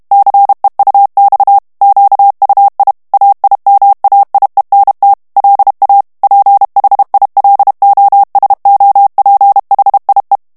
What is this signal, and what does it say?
Morse code, heard as a keyed tone, about 32 wpm, 800 Hz. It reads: CEUX QUI AIMAIENT LA PHILOSOPHIE
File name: FRANCAIS TEXTE CLAIR 2 VITESSE 30 MOTS MINUTES